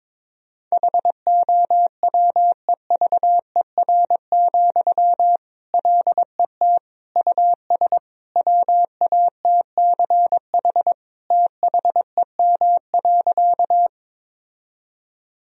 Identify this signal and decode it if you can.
Morse code, heard as a keyed tone, about 22 wpm, 700 Hz. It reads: HOWEVER, LET US WATCH THEM.
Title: Sentences from Top 1000 Words - Review Part 1 - 22wpm